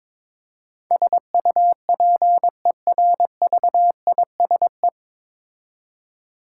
Morse code, heard as a keyed tone, about 22 wpm, 700 Hz. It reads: SUPERVISE
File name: 9 Letter Words - First 3 Letters Given - 22wpm